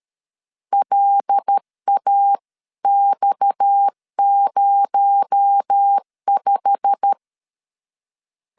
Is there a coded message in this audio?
Yes: Morse code from 0.7 to 7.1 s, heard as a keyed tone, about 13 wpm, 800 Hz. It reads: LAX05